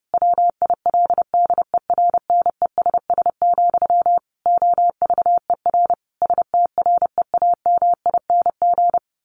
Morse code, heard as a keyed tone, about 30 wpm, 700 Hz. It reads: WILDERNESS, OVER STREAMING
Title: A Sound Of Thunder 30wpm